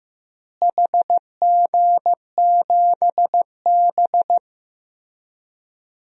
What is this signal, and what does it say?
Morse code, heard as a keyed tone, about 15 wpm, 700 Hz. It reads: HG7B
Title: DXCC-100 Call Signs - 4 Characters - Repeated 2x - 15wpm